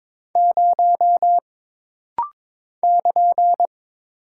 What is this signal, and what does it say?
Morse code, heard as a keyed tone, about 22 wpm, 700 Hz. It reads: 0 <KN>
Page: This rapid-fire timing is geared to push your ICR (Instant Character Recognition) skills.